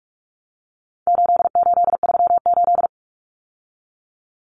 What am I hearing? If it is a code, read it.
Morse code, heard as a keyed tone, about 44 wpm, 700 Hz. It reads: 8838
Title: Numbers - 4-Digit 45wpm